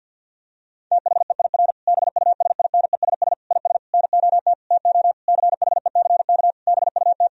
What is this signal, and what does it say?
Morse code, heard as a keyed tone, about 50 wpm, 700 Hz. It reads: THEIR BUSINESS IS NOT TO CHECK BUT
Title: Sentences from Top 1000 Words - Review Part 3 - 50wpm